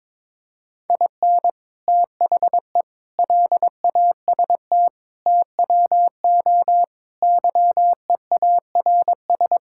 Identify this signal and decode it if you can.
Morse code, heard as a keyed tone, about 22 wpm, 700 Hz. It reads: IN THE LAST TWO YEARS
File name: Build A Phrase - 5-Word Phrases - 22wpm